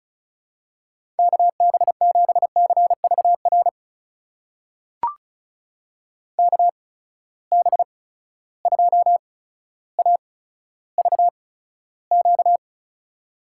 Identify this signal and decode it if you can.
Morse code, heard as a keyed tone, about 36 wpm, spaced as for 15 wpm, 700 Hz. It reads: KB7CVR KB2AVQ
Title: USA Call Signs - 6 Characters - 8x Character Spacing - 35wpm